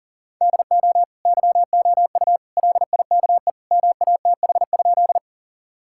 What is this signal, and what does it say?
Morse code, heard as a keyed tone, about 40 wpm, 700 Hz. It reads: DO YOU LIKE MATH?
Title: Sentences from Top 1000 Words - Review Part 6 - 40wpm